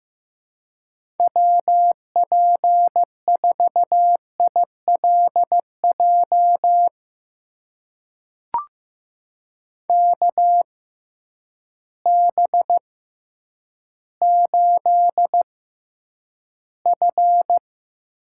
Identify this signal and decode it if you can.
Morse code, heard as a keyed tone, about 15 wpm, spaced as for 9 wpm, 700 Hz. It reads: WP4ILJ KB8F
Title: USA Call Signs - 6 Characters - 6x Character Spacing - 15wpm